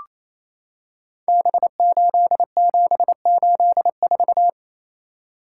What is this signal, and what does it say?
Morse code, heard as a keyed tone, about 27 wpm, 700 Hz. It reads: B8784
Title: Code Groups - 5 Letter:Numbers - 28wpm